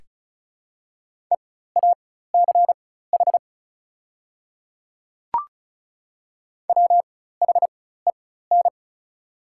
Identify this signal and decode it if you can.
Morse code, heard as a keyed tone, about 36 wpm, spaced as for 16 wpm, 690 Hz. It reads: EACH WHEN
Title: 4x Character Spacing - Top 100 Words - 35wpm